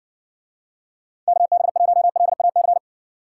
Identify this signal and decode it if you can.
Morse code, heard as a keyed tone, about 62 wpm, 700 Hz. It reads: KB1LAC